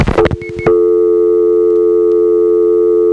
Dial tone (North America), 350 + 440 Hz.